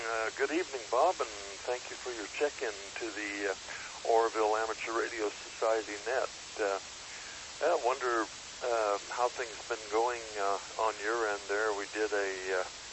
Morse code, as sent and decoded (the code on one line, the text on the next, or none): none